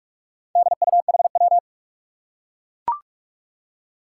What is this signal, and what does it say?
Morse code, heard as a keyed tone, about 44 wpm, 700 Hz. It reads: DUHW
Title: Four Challenging Letter-Number Combinations 45wpm